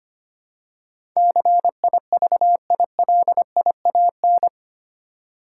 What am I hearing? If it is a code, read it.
Morse code, heard as a keyed tone, about 25 wpm, 700 Hz. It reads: CIVILIAN
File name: Eight Letter Words - 25wpm